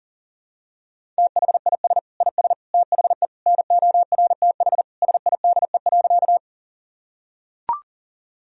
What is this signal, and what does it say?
Morse code, heard as a keyed tone, about 37 wpm, 700 Hz. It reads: THIS IS THE NORTH SIDE.
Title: Sentences from Top 1000 Words - Review Part 1 - 40wpm